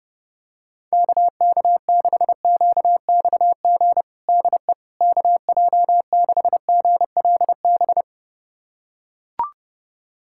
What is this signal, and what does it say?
Morse code, heard as a keyed tone, about 32 wpm, 700 Hz. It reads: KK6QXG DE KJ6GLB
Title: USA Callsign DE Callsign - 6 Characters - 30wpm